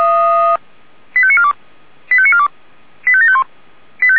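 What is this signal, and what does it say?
Five-tone selective call (ZVEI), heard as quick runs of five tones, tones of about 70 ms. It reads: 85942 85942 85751
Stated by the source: ZVEI five-tone call received by SDR.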